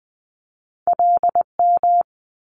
Morse code, heard as a keyed tone, about 19 wpm, 700 Hz. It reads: LM
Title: Multi-letter Phonemes Quick Fire 20wpm